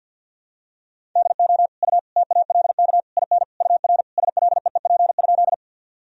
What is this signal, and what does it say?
Morse code, heard as a keyed tone, about 52 wpm, 700 Hz. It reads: DO U TALK IN UR SLEEP?
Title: Sentences from Top 500 Words and Abbreviations 50wpm